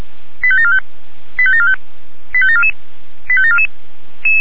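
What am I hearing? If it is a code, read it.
Five-tone selective call (ZVEI), heard as quick runs of five tones, tones of about 70 ms. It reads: 85745 85745 85744 85744
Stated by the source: Received by SDR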